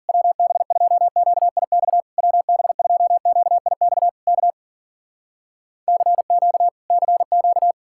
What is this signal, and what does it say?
Morse code, heard as a keyed tone, about 59 wpm, 700 Hz. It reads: WB2QIX WB2QIX K CQ CQ